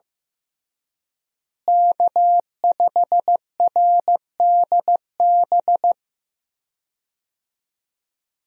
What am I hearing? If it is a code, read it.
Morse code, heard as a keyed tone, about 15 wpm, 700 Hz. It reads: K5RDB
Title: USA Call Signs - 5 Characters - Repeated 3x - 15wpm